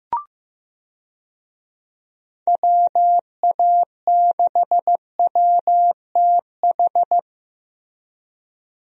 Morse code, heard as a keyed tone, about 15 wpm, 700 Hz. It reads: WA6WTH
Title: USA Call Signs - 6 Characters - Repeated 3x - 15wpm